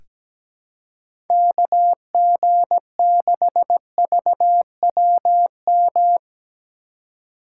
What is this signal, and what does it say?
Morse code, heard as a keyed tone, about 17 wpm, 700 Hz. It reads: KG6VWM
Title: USA Call Signs - 6 Characters - 3x Character Spacing - 17wpm